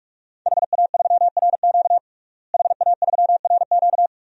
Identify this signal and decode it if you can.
Morse code, heard as a keyed tone, about 45 wpm, 700 Hz. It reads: HA3RQ HA3RQ
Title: DXCC-100 Call Signs - 5 Characters - Repeated 2x - 45wpm